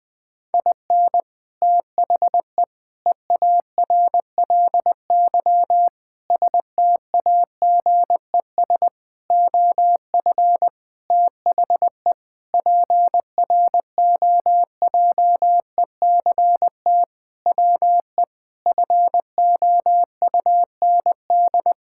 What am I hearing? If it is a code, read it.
Morse code, heard as a keyed tone, about 20 wpm, 700 Hz. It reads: IN THE EARLY STAGES OF THE PROJECT WE FOUND
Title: Sentences with 6-Word Phrases - Spoken Beginning - 20wpm